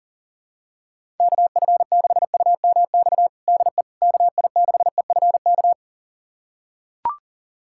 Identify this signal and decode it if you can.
Morse code, heard as a keyed tone, about 40 wpm, 700 Hz. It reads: KF6UMX DE KI6EFK
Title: USA Callsign DE Callsign - 6 Characters - 40wpm